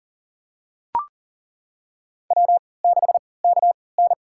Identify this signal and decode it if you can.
Morse code, heard as a keyed tone, about 41 wpm, spaced as for 23 wpm, 700 Hz. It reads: W6KN
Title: USA Call Signs - 4 Characters - 3x Character Spacing - 40wpm